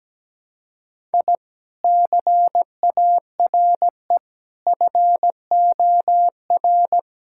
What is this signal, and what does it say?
Morse code, heard as a keyed tone, about 17 wpm, 700 Hz. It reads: I CARE FOR
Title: Sentences from Top 1000 Words - Review Part 6 - 17wpm